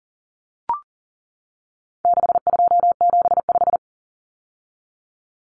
Morse code, heard as a keyed tone, about 37 wpm, 700 Hz. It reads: 6275
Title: Numbers - 4-Digit 40wpm